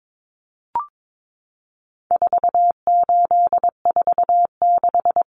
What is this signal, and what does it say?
Morse code, heard as a keyed tone, about 22 wpm, 700 Hz. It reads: 4846
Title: Common Cut Numbers - 4-Digit 22wpm